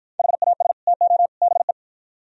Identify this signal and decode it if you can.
Morse code, heard as a keyed tone, about 52 wpm, 700 Hz. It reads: HAS TO BE